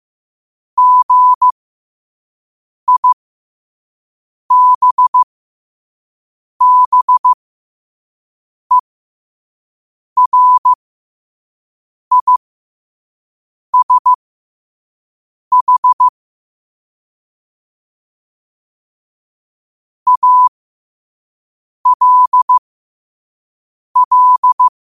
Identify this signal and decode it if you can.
Morse code, heard as a keyed tone, about 15 wpm, spaced as for 5 wpm, 1000 Hz. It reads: GIBBERISH ALL